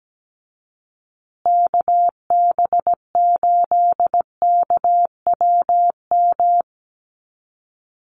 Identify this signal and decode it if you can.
Morse code, heard as a keyed tone, about 17 wpm, 700 Hz. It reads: KB8KWM
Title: USA Call Signs Repeated 3x 17wpm